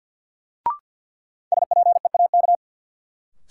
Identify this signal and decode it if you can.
Morse code, heard as a keyed tone, about 52 wpm, 700 Hz. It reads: SPEAK
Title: Sentences from Top 1000 Words - Review Part 2 - New Word Drill - 50wpm